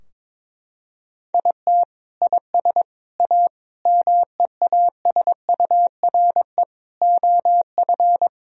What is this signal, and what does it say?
Morse code, heard as a keyed tone, about 21 wpm, 700 Hz. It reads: IT IS A MEASURE OF
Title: Sentences from Top 1000 Words - Review Part 2 - 22wpm